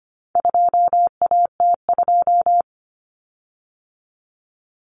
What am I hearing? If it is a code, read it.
Morse code, heard as a keyed tone, about 25 wpm, 700 Hz. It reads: 2AT2
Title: Common Cut Numbers - 4-Digit 25wpm